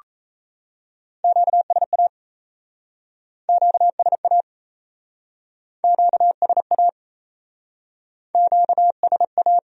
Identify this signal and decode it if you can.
Morse code, heard as a keyed tone, about 38 wpm, 700 Hz. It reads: QSA QSA QSA QSA